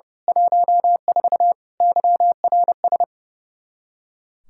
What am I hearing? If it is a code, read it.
Morse code, heard as a keyed tone, about 31 wpm, 700 Hz. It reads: 14 YRS
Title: Practice QSO - Exchange 3 - More Info - 30wpm